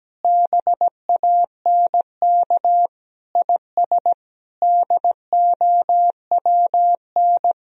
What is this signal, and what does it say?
Morse code, heard as a keyed tone, about 17 wpm, 700 Hz. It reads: BANK IS DOWN
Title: Sentences from Top 1000 Words - Review Part 4 - 17wpm